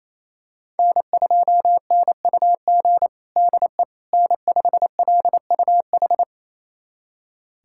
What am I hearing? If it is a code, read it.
Morse code, heard as a keyed tone, about 27 wpm, 700 Hz. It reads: N2NUG DE N5LUH
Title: USA Callsign DE Callsign - 5 Characters - 28wpm